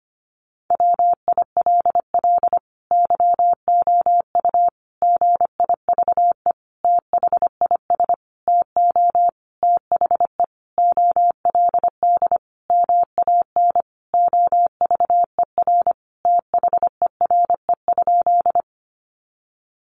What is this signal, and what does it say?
Morse code, heard as a keyed tone, about 25 wpm, 700 Hz. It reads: WILL YOU GIVE THIS TO THE OLD MAN OVER THERE?
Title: Sentences from Top 200 Words 25wpm